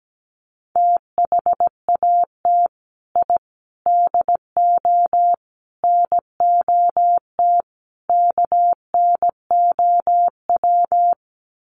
Morse code, heard as a keyed tone, about 17 wpm, 700 Hz. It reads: THAT I DO NOT KNOW
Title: Sets of 5 Words Encore2 17wpm